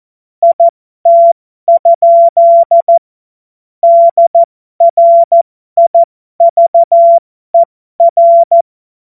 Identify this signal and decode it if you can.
Morse code, heard as a keyed tone, about 14 wpm, spaced as for 12 wpm, 680 Hz. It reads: IT? DRIVER